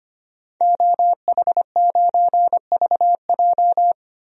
Morse code, heard as a keyed tone, about 25 wpm, 700 Hz. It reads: OH9VJ
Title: DXCC-100 Call Signs - 5 Characters - 25wpm